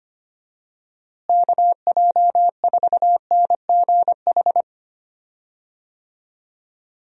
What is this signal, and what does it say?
Morse code, heard as a keyed tone, about 25 wpm, 700 Hz. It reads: KJ4NGH
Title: USA Call Signs - 6 Characters - Repeated 2x - 25wpm